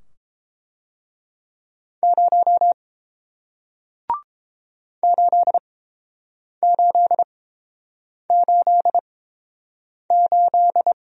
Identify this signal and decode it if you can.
Morse code, heard as a keyed tone, about 30 wpm, spaced as for 7 wpm, 700 Hz. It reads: 0 8888